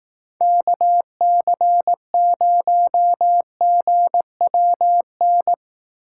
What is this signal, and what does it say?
Morse code, heard as a keyed tone, about 18 wpm, 700 Hz. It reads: KC0GWN
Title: Speed Racing - USA 2x3 Call Signs 15wpm